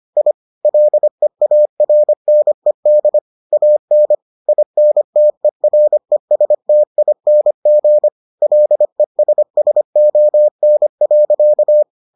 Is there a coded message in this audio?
Yes: Morse code from 0.2 to 11.8 s, heard as a keyed tone, about 25 wpm, 600 Hz. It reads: I LEARNED AN INTERESTING LESSON.